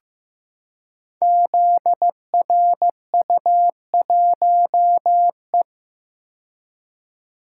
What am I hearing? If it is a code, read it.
Morse code, heard as a keyed tone, about 15 wpm, 700 Hz. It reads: ZRU1E
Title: Code Groups - 5 Letter:Numbers - 15wpm